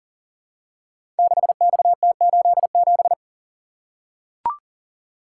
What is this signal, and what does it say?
Morse code, heard as a keyed tone, about 40 wpm, 690 Hz. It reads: 6XT87
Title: Code Groups - 5 Letter:Numbers - 40wpm